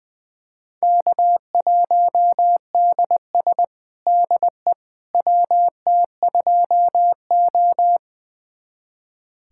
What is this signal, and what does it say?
Morse code, heard as a keyed tone, about 20 wpm, 700 Hz. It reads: K1DS DE WT2O
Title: USA Callsign DE Callsign - 4 Characters - 20wpm